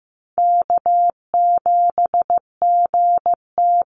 Morse code, heard as a keyed tone, about 15 wpm, 700 Hz. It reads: K7GT
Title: Call Signs Repeated 2x - 15wpm